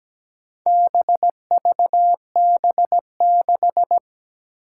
Morse code, heard as a keyed tone, about 17 wpm, 700 Hz. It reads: BVB6
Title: Four Challenging Letter-Number Combinations 17wpm